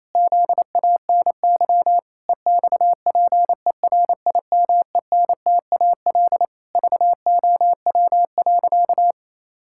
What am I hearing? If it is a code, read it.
Morse code, heard as a keyed tone, about 28 wpm, 700 Hz. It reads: ZANY EXPERIMENTAL VOW.
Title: Pangram Challenge v2 - 28wpm